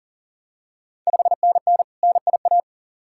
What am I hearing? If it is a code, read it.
Morse code, heard as a keyed tone, about 40 wpm, 690 Hz. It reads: 5NN NIA
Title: Ontario QSO Party Exchange 40wpm